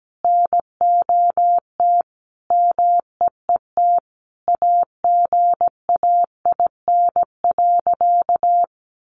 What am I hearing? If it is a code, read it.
Morse code, heard as a keyed tone, about 17 wpm, 700 Hz. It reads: NOT MEET AGAIN.
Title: Sentences from Top 1100 Words 17wpm